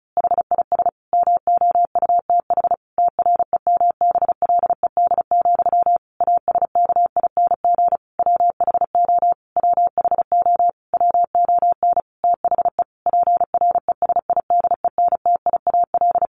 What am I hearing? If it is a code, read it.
Morse code, heard as a keyed tone, about 34 wpm, 700 Hz. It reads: HIS MOUTH TREMBLED, ASKING WHO WHO WON THE PRESIDENTIAL
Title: A Sound Of Thunder 35wpm